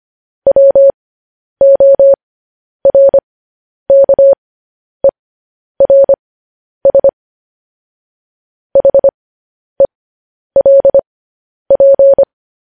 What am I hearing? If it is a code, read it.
Morse code, heard as a keyed tone, about 25 wpm, spaced as for 10 wpm, 550 Hz. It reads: WORKERS HELP
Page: news-headlines-in-morse-code-at-10-wpm